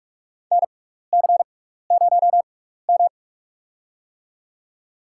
Morse code, heard as a keyed tone, about 46 wpm, spaced as for 16 wpm, 700 Hz. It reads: NC0M
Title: USA Call Signs - 4 Characters - 6x Character Spacing - 45wpm